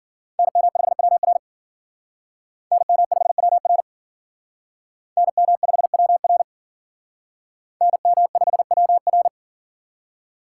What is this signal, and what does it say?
Morse code, heard as a keyed tone, about 51 wpm, 700 Hz. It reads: NM5WR NM5WR NM5WR NM5WR